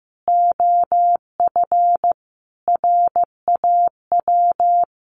Morse code, heard as a keyed tone, about 15 wpm, 700 Hz. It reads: OF RAW
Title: Sentences from Top 1500 Words 15wpm